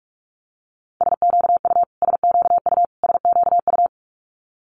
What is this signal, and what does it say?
Morse code, heard as a keyed tone, about 46 wpm, 690 Hz. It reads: SQU SQU SQU
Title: Multi-letter Phonemes Repeated 3x 45wpm